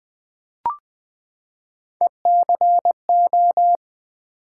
Morse code, heard as a keyed tone, about 20 wpm, 700 Hz. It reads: ECO
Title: Build A Word - 3 Letters At A Time - 8-Letter Words - 20wpm